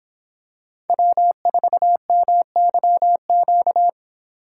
Morse code, heard as a keyed tone, about 26 wpm, 700 Hz. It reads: W4MYQ
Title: Speed Racing - USA 1x3 Call Signs 22wpm